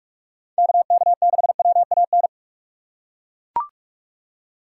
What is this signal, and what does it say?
Morse code, heard as a keyed tone, about 40 wpm, 700 Hz. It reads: KK6WAN